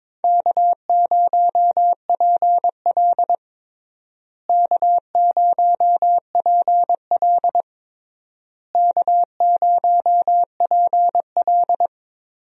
Morse code, heard as a keyed tone, about 22 wpm, 700 Hz. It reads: K0PL K0PL K0PL
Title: USA Call Signs - 4 Characters - Repeated 3x - 22wpm